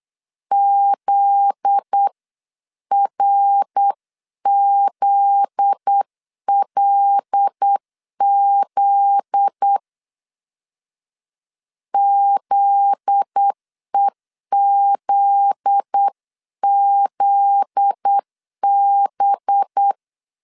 Morse code, heard as a keyed tone, about 8 wpm, 800 Hz. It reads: ZRZLZ ZEZZB
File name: LEZIONI / 7 GF9Z / 7 GF9Z 8wpm 3213